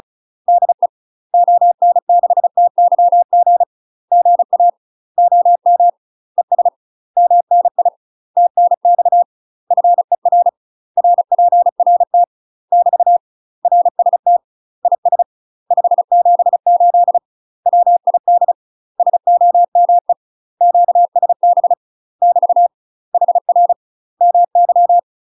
Morse code, heard as a keyed tone, about 34 wpm, 700 Hz. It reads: DE ON6TYG GA OM ES MNI TNX FER RPRT <BT> RST IS 578 WID SOME QSB <BT> HR MY
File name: UK MASTER QSOs TRAINING / 35WPM / UK MASTER QSOs 0028